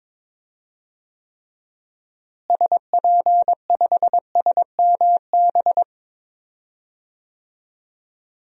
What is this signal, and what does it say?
Morse code, heard as a keyed tone, about 22 wpm, 700 Hz. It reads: SP5SMB
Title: DXCC-100 Call Signs - 6 Characters - Repeated 2x - 22wpm